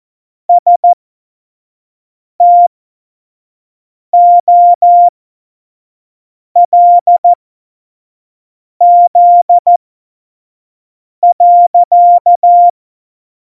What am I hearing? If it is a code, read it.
Morse code, heard as a keyed tone, about 14 wpm, spaced as for 5 wpm, 700 Hz. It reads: STOLZ.